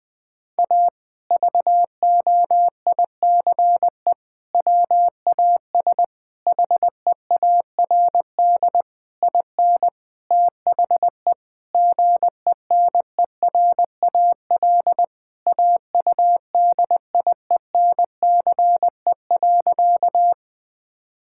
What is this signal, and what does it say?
Morse code, heard as a keyed tone, about 20 wpm, 700 Hz. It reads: A VOICE WAS HEARD IN THE GENERAL AUDIENCE.
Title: Sentences from Top 1000 Words - Review Part 5 - 20wpm